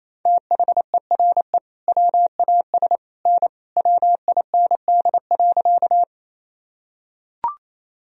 Morse code, heard as a keyed tone, about 28 wpm, 700 Hz. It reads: THERE WAS N WIND.
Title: Sentences from Top 400 Words and Abbreviations 28wpm